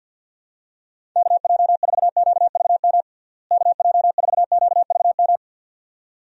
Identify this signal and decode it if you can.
Morse code, heard as a keyed tone, about 52 wpm, 700 Hz. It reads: KJ4QVM KJ4QVM
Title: USA Call Signs - 6 Characters - Repeated 2x - 50wpm